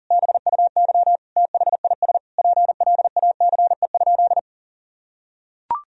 Morse code, heard as a keyed tone, about 40 wpm, 700 Hz. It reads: BUY THIS PLACE?
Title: Sentences from Top 1000 Words - Review Part 4 - 40wpm